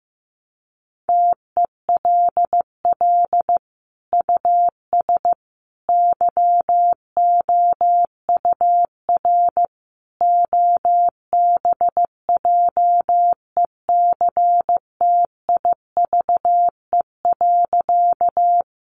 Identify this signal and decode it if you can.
Morse code, heard as a keyed tone, about 15 wpm, 700 Hz. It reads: TELL US YOUR OBJECTIVE.
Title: Sentences from Top 1200 Words 15wpm